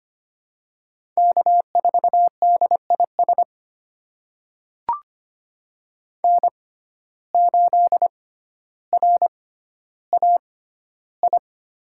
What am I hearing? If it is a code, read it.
Morse code, heard as a keyed tone, about 25 wpm, spaced as for 13 wpm, 700 Hz. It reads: K4DIS N8RAI
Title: USA Call Signs - 5 Characters - 6x Character Spacing - 25wpm